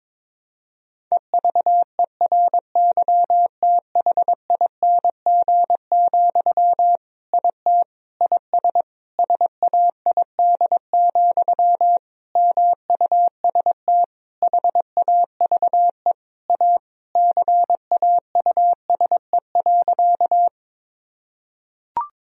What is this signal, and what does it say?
Morse code, heard as a keyed tone, about 21 wpm, 700 Hz. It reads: EVERYTHING, IT IS SAID, MUST HAVE A CAUSE.
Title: Sentences from Top 1000 Words - Review Part 1 - 22wpm